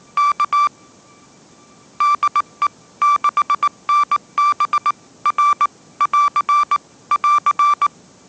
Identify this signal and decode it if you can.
Morse code, heard as a keyed tone, about 19 wpm, 1170 Hz. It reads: K DE 6NB R <AR> <AR>